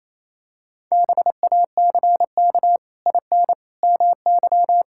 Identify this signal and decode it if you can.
Morse code, heard as a keyed tone, about 27 wpm, 700 Hz. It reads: BACK IN MY
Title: Pangram Challenge v2 - 28wpm